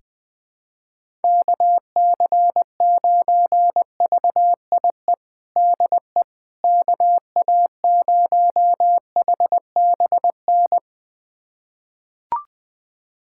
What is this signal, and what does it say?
Morse code, heard as a keyed tone, about 20 wpm, 700 Hz. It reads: KC9VIE DE KA0HBN
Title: USA Callsign DE Callsign - 6 Characters - 20wpm